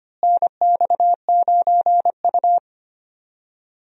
Morse code, heard as a keyed tone, about 25 wpm, 700 Hz. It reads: NX9U
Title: USA Call Signs - 4 Characters - Repeated 3x - 25wpm